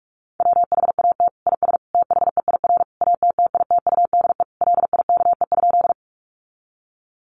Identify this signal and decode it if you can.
Morse code, heard as a keyed tone, about 44 wpm, 700 Hz. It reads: WHAT IS THEIR ATTITUDE LIKE?
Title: Sentences from Top 1300 Words 45wpm